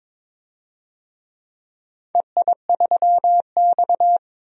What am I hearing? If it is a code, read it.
Morse code, heard as a keyed tone, about 22 wpm, 690 Hz. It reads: EI3X
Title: DXCC-100 Call Signs - 4 Characters - 22wpm